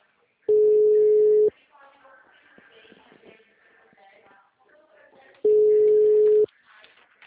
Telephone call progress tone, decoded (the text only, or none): ringback tone